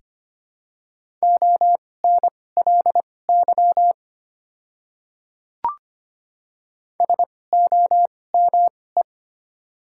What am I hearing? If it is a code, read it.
Morse code, heard as a keyed tone, about 25 wpm, spaced as for 18 wpm, 700 Hz. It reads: ONLY SOME
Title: 2x Character Spacing - Top 300 Words - 25wpm